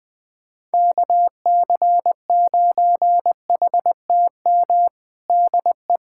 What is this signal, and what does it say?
Morse code, heard as a keyed tone, about 21 wpm, 700 Hz. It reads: KC9HTM DE
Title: USA Callsign DE Callsign 20wpm